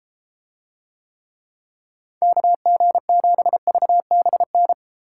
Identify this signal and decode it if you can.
Morse code, heard as a keyed tone, about 32 wpm, 700 Hz. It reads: KG7VBN